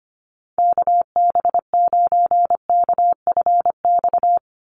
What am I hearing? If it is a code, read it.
Morse code, heard as a keyed tone, about 25 wpm, 700 Hz. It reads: KB9KFX
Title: USA Call Signs 25wpm